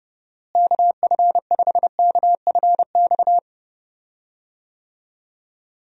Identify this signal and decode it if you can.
Morse code, heard as a keyed tone, about 30 wpm, 700 Hz. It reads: KF5KFX
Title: USA Call Signs - 6 Characters - Repeated 3x - 30wpm